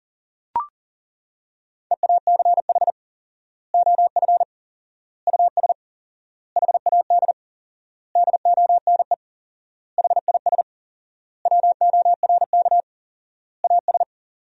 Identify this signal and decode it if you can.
Morse code, heard as a keyed tone, about 37 wpm, 700 Hz. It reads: EACH OF US HAD DONE HIS WORK AS
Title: 4x Word-Spacing - Sentences from Top 500 Words - 40wpm